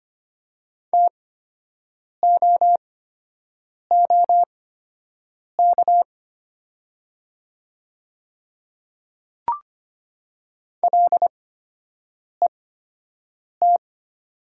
Morse code, heard as a keyed tone, about 25 wpm, spaced as for 7 wpm, 700 Hz. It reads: TOOK LET
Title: 8x Character Spacing - Top 300 Words - 25wpm